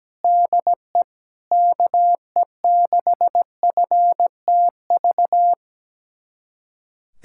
Morse code, heard as a keyed tone, about 17 wpm, 700 Hz. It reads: DE KE6FTV
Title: USA Callsign DE Callsign 17wpm